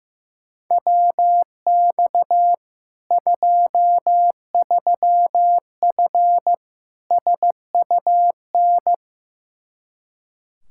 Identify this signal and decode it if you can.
Morse code, heard as a keyed tone, about 15 wpm, 700 Hz. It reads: WX 23F SUN
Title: Practice QSO - Exchange 2 - Info - 15wpm